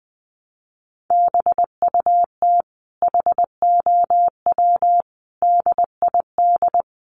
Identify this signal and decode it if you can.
Morse code, heard as a keyed tone, about 20 wpm, 700 Hz. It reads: BUT HOW DID
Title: Sentences from Top 1000 Words 20wpm